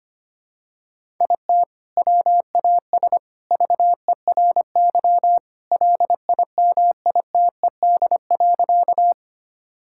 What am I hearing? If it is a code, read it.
Morse code, heard as a keyed tone, about 25 wpm, 700 Hz. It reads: IT WAS VERY LIMITED.
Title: Sentences from Top 1000 Words - Review Part 6 - 25wpm